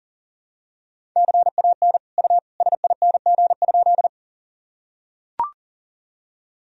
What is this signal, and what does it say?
Morse code, heard as a keyed tone, about 40 wpm, 700 Hz. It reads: CAN U SING?
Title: Sentences from Top 400 Words and Abbreviations 40wpm